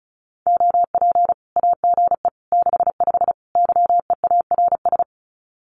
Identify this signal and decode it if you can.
Morse code, heard as a keyed tone, about 34 wpm, 700 Hz. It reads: OP AGE 65 YEARS
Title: Common QSO Phrases 35wpm